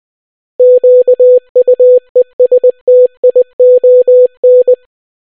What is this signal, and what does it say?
Morse code, heard as a keyed tone, about 20 wpm, 500 Hz. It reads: QUESTION